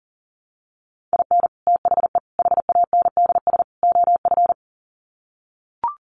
Morse code, heard as a keyed tone, about 40 wpm, 700 Hz. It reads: IN THE HANDS OF
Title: Sets of 4 Words 40wpm